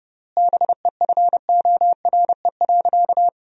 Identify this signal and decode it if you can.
Morse code, heard as a keyed tone, about 30 wpm, 700 Hz. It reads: BEFORE.
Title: 3x Word-Spacing - Sentences from Top 400 Words - 30wpm